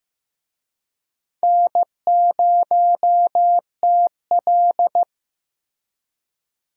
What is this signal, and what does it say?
Morse code, heard as a keyed tone, about 15 wpm, 700 Hz. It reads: N0TL
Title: USA Call Signs - 4 Characters - 15wpm